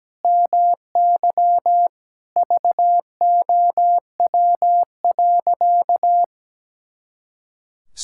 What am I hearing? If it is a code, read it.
Morse code, heard as a keyed tone, about 17 wpm, 700 Hz. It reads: MY VOW.
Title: Pangram Challenge v2 - 17wpm